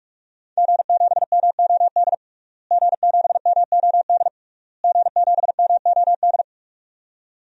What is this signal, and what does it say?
Morse code, heard as a keyed tone, about 46 wpm, 700 Hz. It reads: G7MOD G7MOD G7MOD
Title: DXCC-100 Call Signs - 5 Characters - Repeated 3x - 45wpm